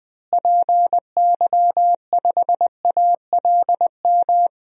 Morse code, heard as a keyed tone, about 21 wpm, 700 Hz. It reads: PY5ALM
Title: DXCC-100 Call Signs - 6 Characters - Repeated 3x - 20wpm